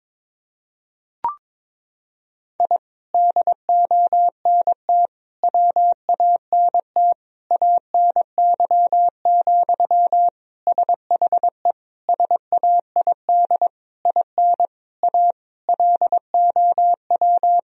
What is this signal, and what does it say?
Morse code, heard as a keyed tone, about 22 wpm, 700 Hz. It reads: I DONT WANT ANY, SHE SAID IN A LOW
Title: Sentences from Top 1000 Words - Review Part 1 - 22wpm